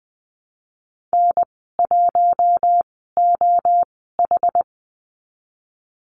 Morse code, heard as a keyed tone, about 19 wpm, spaced as for 14 wpm, 700 Hz. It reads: N1OH
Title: 193 Lesson 31g - Letter v - Call Signs 20wpm